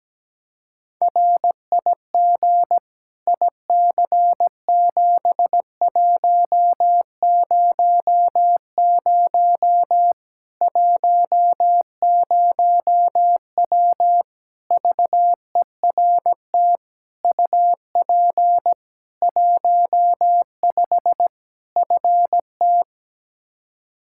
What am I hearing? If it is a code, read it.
Morse code, heard as a keyed tone, about 17 wpm, 700 Hz. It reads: RIG IC7100 10W VERT UP 15 FT
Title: Practice QSO - Exchange 2 - Info - 17wpm